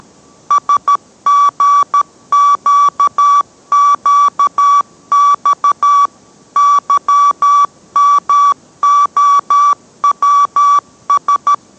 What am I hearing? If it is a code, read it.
Morse code, heard as a keyed tone, about 13 wpm, 1150 Hz. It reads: SGQQX YMOWS